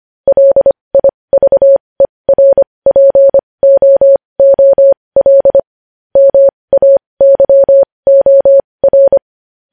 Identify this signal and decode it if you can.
Morse code, heard as a keyed tone, about 25 wpm, spaced as for 20 wpm, 550 Hz. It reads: LIVERPOOL MAYOR
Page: news-headlines-in-morse-code-at-20-wpm